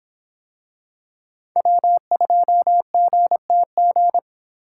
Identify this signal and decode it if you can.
Morse code, heard as a keyed tone, about 23 wpm, 700 Hz. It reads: W2GTG